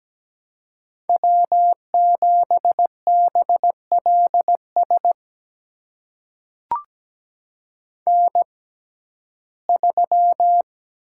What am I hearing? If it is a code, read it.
Morse code, heard as a keyed tone, about 17 wpm, spaced as for 12 wpm, 700 Hz. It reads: W7BLS N3
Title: USA Call Signs - 5 Characters - 6x Character Spacing - 17wpm